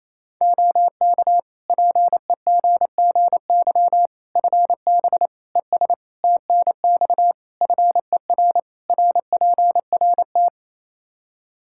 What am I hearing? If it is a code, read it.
Morse code, heard as a keyed tone, about 27 wpm, 700 Hz. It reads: OK PEGGY FB ES TNX FER RPRT
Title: Practice QSO - Exchange 2 - Info - 28wpm